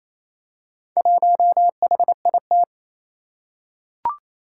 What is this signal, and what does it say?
Morse code, heard as a keyed tone, about 29 wpm, 700 Hz. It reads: 1HIT
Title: Four Challenging Letter-Number Combinations 28wpm